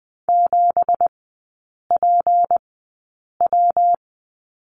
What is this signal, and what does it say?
Morse code, heard as a keyed tone, about 20 wpm, spaced as for 8 wpm, 700 Hz. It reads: 7PW
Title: 166 Lesson 27e - Number 7 - Three Characters 20wpm